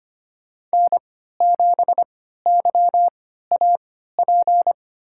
Morse code, heard as a keyed tone, about 25 wpm, spaced as for 14 wpm, 700 Hz. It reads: N7YAP
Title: USA Call Signs - 5 Characters - 3x Character Spacing - 25wpm